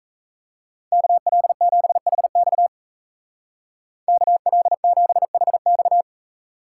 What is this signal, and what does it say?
Morse code, heard as a keyed tone, about 35 wpm, 700 Hz. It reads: KL7HX KL7HX
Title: Speed Racing - USA 2x2 Call Signs 28wpm